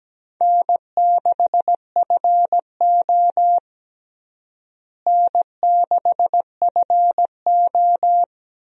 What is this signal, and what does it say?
Morse code, heard as a keyed tone, about 17 wpm, 700 Hz. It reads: N6FO N6FO
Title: USA Call Signs - 4 Characters - Repeated 2x - 17wpm